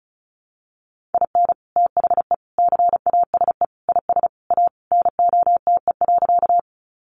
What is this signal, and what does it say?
Morse code, heard as a keyed tone, about 34 wpm, 700 Hz. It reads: IN THE CASE IS A NOTE.
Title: Sentences from Top 700 Words 35wpm